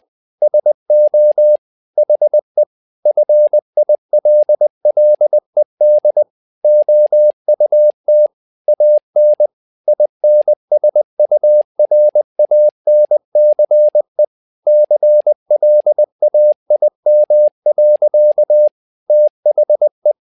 Morse code, heard as a keyed tone, about 20 wpm, 600 Hz. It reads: SO HE FILLED OUT AN INSURANCE CLAIM. THE
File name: learning center / tower trouble / tower trouble 20wpm